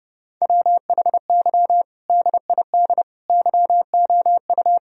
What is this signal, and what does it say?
Morse code, heard as a keyed tone, about 30 wpm, 700 Hz. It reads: WHY DID YOU
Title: Sentences from Top 1000 Words - Review Part 3 - 30wpm